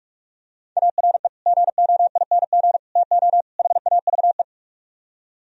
Morse code, heard as a keyed tone, about 46 wpm, 700 Hz. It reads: ARE GOING TO HAVE